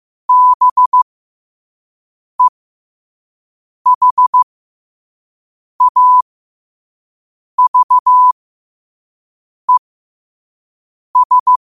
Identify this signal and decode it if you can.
Morse code, heard as a keyed tone, about 15 wpm, spaced as for 5 wpm, 1000 Hz. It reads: BEHAVES
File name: qotdfiles / 5wpm